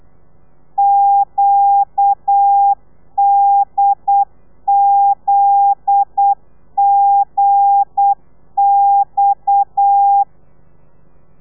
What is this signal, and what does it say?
Morse code, heard as a keyed tone, about 8 wpm, 800 Hz. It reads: QDZGX
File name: telegrafia / i2tfj / Lez9 35